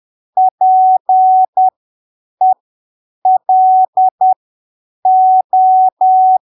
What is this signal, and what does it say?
Morse code, heard as a keyed tone, about 10 wpm, spaced as for 7 wpm, 750 Hz. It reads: PELO
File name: arquivos / texto / texto 05 10ppm